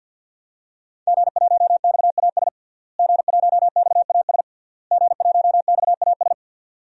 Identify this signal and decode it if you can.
Morse code, heard as a keyed tone, about 52 wpm, 700 Hz. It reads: G1XAS G1XAS G1XAS
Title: DXCC-100 Call Signs - 5 Characters - Repeated 3x - 50wpm